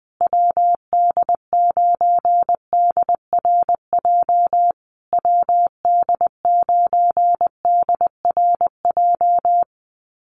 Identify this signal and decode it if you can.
Morse code, heard as a keyed tone, about 19 wpm, 700 Hz. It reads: WD9DRJ WD9DRJ
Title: USA Call Signs Repeated 3x 20wpm